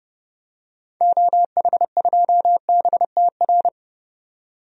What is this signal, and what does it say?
Morse code, heard as a keyed tone, about 30 wpm, 700 Hz. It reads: OH2BTR
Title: DXCC-100 Call Signs - 6 Characters - 30wpm